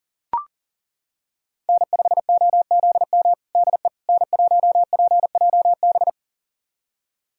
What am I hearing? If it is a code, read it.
Morse code, heard as a keyed tone, about 40 wpm, 700 Hz. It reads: N5OZM DE N1PJB
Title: USA Callsign DE Callsign - 5 Characters - 40wpm